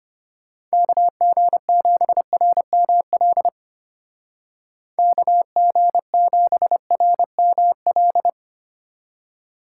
Morse code, heard as a keyed tone, about 30 wpm, 700 Hz. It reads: KG7RML KG7RML
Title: Speed Racing - USA 2x3 Call Signs 25wpm